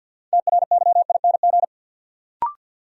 Morse code, heard as a keyed tone, about 50 wpm, 700 Hz. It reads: TRYING
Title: Sets of 2 Words - New Word Drill - 50wpm